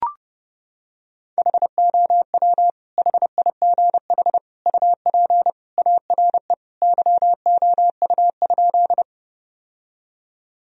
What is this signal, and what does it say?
Morse code, heard as a keyed tone, about 32 wpm, 700 Hz. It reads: HOW HIGH UP ARE YOU?
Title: Sentences from Top 1000 Words - Review Part 1 - 30wpm